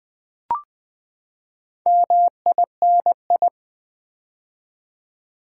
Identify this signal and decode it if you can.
Morse code, heard as a keyed tone, about 20 wpm, 700 Hz. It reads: MINI
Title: Build A Word - 4 Letters At A Time - 8-Letter Words - 20wpm